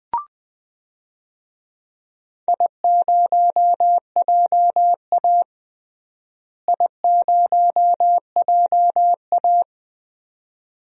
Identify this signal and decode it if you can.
Morse code, heard as a keyed tone, about 20 wpm, 700 Hz. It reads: I0JA I0JA
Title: DXCC-100 Call Signs - 4 Characters - Repeated 2x - 20wpm